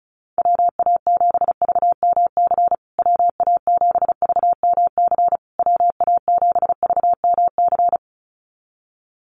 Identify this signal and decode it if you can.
Morse code, heard as a keyed tone, about 34 wpm, 700 Hz. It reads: WA7VMC WA7VMC WA7VMC
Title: USA Call Signs Repeated 3x 35wpm